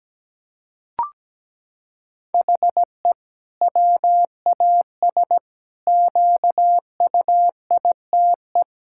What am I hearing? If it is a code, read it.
Morse code, heard as a keyed tone, about 17 wpm, 700 Hz. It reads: HE WAS QUITE
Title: Sentences from Top 1000 Words - Review Part 5 - 17wpm